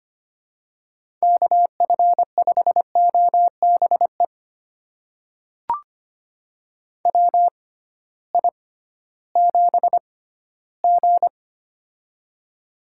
Morse code, heard as a keyed tone, about 25 wpm, spaced as for 15 wpm, 700 Hz. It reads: KF5OBE WI7G
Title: USA Call Signs - 6x Character Spacing - 25wpm